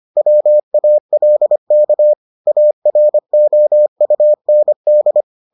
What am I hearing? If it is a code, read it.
Morse code, heard as a keyed tone, about 25 wpm, 600 Hz. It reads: WALK AROUND